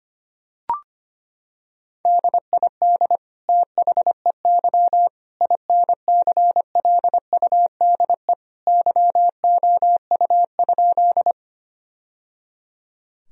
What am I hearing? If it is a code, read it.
Morse code, heard as a keyed tone, about 25 wpm, 700 Hz. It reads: DID THEY INCLUDE YOU?
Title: Sentences from Top 1000 Words - Review Part 3 - 25wpm